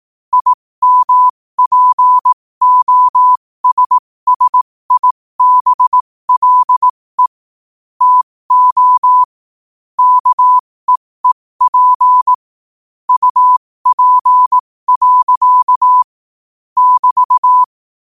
Morse code, heard as a keyed tone, about 18 wpm, spaced as for 15 wpm, 1000 Hz. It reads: IMPOSSIBLE TO KEEP UP. <BT>
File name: qotdfiles / 15wpm